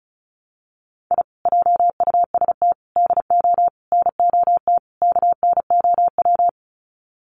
Morse code, heard as a keyed tone, about 34 wpm, 700 Hz. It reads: I JUST DO NOT KNOW
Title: Sets of 5 Words Encore 35wpm